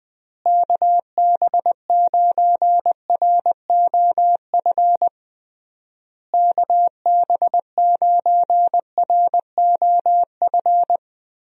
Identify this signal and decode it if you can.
Morse code, heard as a keyed tone, about 20 wpm, 700 Hz. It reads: KB9ROF KB9ROF
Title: USA Call Signs - 6 Characters - Repeated 3x - 20wpm